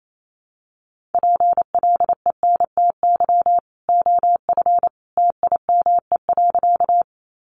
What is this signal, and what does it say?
Morse code, heard as a keyed tone, about 28 wpm, 700 Hz. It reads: PLENTY OF TIME.
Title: Sentences from Top 1300 Words 28wpm